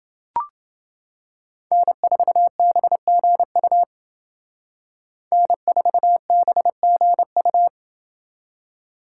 Morse code, heard as a keyed tone, about 28 wpm, 700 Hz. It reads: N4BGU N4BGU